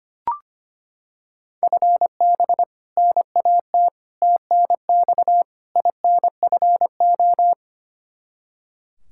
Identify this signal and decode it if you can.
Morse code, heard as a keyed tone, about 25 wpm, 700 Hz. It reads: FB NAT TNX INFO
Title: Practice QSO - Exchange 3 - More Info - 25wpm